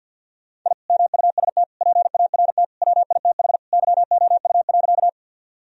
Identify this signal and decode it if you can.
Morse code, heard as a keyed tone, about 50 wpm, 700 Hz. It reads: I MUST PART WITH YOU.
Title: Sentences from Top 1000 Words - Review Part 1 - 50wpm